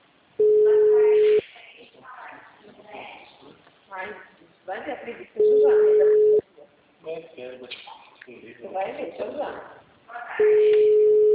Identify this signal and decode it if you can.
Ringback tone (Europe, 425 Hz), with speech.